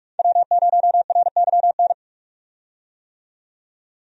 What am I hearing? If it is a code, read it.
Morse code, heard as a keyed tone, about 46 wpm, 700 Hz. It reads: W0RYN
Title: USA Call Signs - 5 Characters - 45wpm